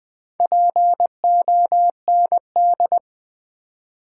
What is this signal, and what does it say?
Morse code, heard as a keyed tone, about 20 wpm, 700 Hz. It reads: POND
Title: Build A Word - 2 Letters At A Time - 4-Letter Words - 20wpm